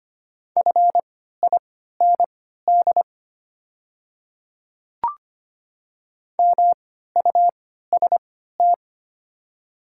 Morse code, heard as a keyed tone, about 25 wpm, spaced as for 14 wpm, 700 Hz. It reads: FIND MUST
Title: 3x Character Spacing - Top 200 Words - 25wpm